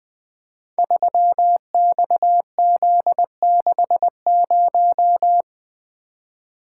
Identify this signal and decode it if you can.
Morse code, heard as a keyed tone, about 20 wpm, 700 Hz. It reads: 3XZ60
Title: Code Groups - 5 Letter:Numbers - 20wpm